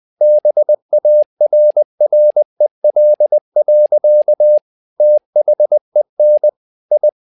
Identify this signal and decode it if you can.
Morse code, heard as a keyed tone, about 20 wpm, 600 Hz. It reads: BARREL. THEN I
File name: learning center / tower trouble / tower trouble 20wpm